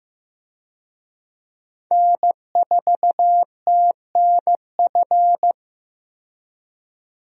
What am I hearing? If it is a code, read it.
Morse code, heard as a keyed tone, about 15 wpm, 700 Hz. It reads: N4TNF
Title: USA Call Signs - 5 Characters - 15wpm